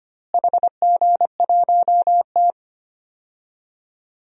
Morse code, heard as a keyed tone, about 25 wpm, 700 Hz. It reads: HG1T
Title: DXCC-100 Call Signs - 4 Characters - Repeated 3x - 25wpm